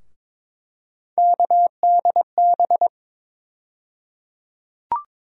Morse code, heard as a keyed tone, about 22 wpm, 700 Hz. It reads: KDB
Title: USA Call Sign Postfixes - 3 Letters 22wpm